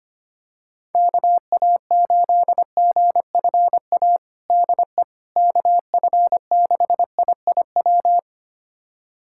Morse code, heard as a keyed tone, about 25 wpm, 700 Hz. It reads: KA8GFA DE KF6IIW
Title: USA Callsign DE Callsign 25wpm